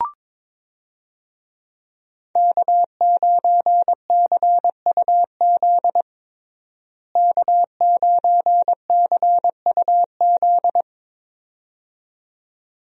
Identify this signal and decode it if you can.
Morse code, heard as a keyed tone, about 22 wpm, 700 Hz. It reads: K9CUZ K9CUZ
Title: USA Call Signs - 5 Characters - Repeated 2x - 22wpm